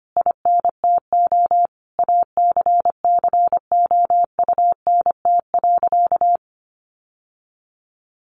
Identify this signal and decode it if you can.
Morse code, heard as a keyed tone, about 25 wpm, 700 Hz. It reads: INTO ACCOUNT.
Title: Sentences from Top 1300 Words 25wpm